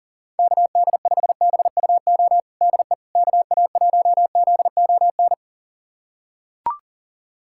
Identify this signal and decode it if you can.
Morse code, heard as a keyed tone, about 37 wpm, 700 Hz. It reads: KD5BUO DE KA1ZON
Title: USA Callsign DE Callsign - 6 Characters - 40wpm